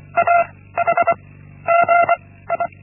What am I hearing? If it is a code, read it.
Morse code, heard as a keyed tone, about 24 wpm, 680 Hz. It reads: A H G I
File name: Linksmakalnis-1